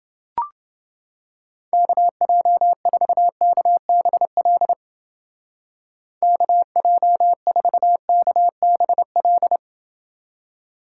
Morse code, heard as a keyed tone, about 28 wpm, 700 Hz. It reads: KJ4KBL KJ4KBL